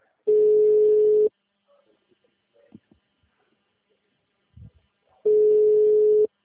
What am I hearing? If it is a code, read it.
Ringback tone (Europe, 425 Hz).
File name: out- -2032- - - .82